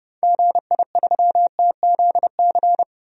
Morse code, heard as a keyed tone, about 30 wpm, 700 Hz. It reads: GI3TZC
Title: DXCC-100 Call Signs - 6 Characters - 30wpm